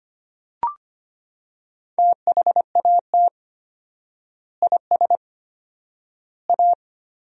Morse code, heard as a keyed tone, about 25 wpm, 700 Hz. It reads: THAT IS A
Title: 4x Word-Spacing - Sentences from Top 500 Words - 25wpm